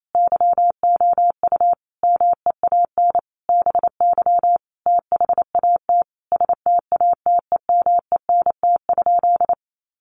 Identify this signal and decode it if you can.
Morse code, heard as a keyed tone, about 28 wpm, 700 Hz. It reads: YOU MEAN BY THAT STATEMENT?
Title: Sentences from Top 1200 Words 28wpm